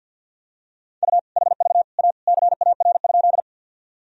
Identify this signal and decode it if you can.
Morse code, heard as a keyed tone, about 52 wpm, 700 Hz. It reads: U HV A CAR?
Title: Sentences from Top 300 Words and Abbreviations 50wpm